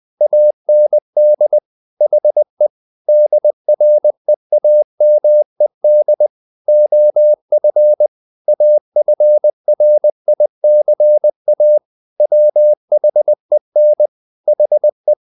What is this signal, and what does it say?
Morse code, heard as a keyed tone, about 20 wpm, 600 Hz. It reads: AND HE DREAMED OF AFRICA WHEN HE
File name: the old man and the sea / the old man and the sea part 1 20wpm